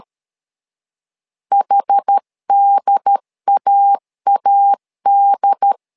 Morse code, heard as a keyed tone, about 13 wpm, 800 Hz. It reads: HDAAD
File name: LEZIONI / 2 UHDN / 2 UHD 12wpm 1540